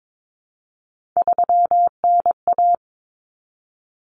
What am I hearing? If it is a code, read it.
Morse code, heard as a keyed tone, about 22 wpm, 700 Hz. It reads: 3NA
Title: Common Cut Numbers - 3-Digit 22wpm